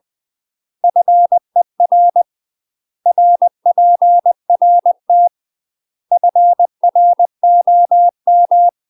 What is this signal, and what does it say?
Morse code, heard as a keyed tone, about 20 wpm, 700 Hz. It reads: FER RPRT FROM
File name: UK MASTER QSOs TRAINING / 20WPM / UK MASTER QSOs 0071